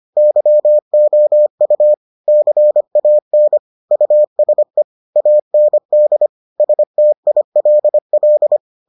Morse code, heard as a keyed tone, about 25 wpm, 600 Hz. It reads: YOU CAN USE AND STILL